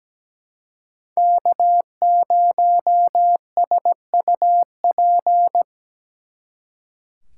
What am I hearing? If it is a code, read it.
Morse code, heard as a keyed tone, about 20 wpm, 700 Hz. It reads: K0SUP
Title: Speed Racing - USA 1x3 Call Signs 17wpm